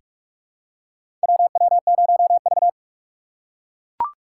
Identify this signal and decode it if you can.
Morse code, heard as a keyed tone, about 41 wpm, 700 Hz. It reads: WW0V
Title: USA Call Signs - 4 Characters - 6x Character Spacing - 45wpm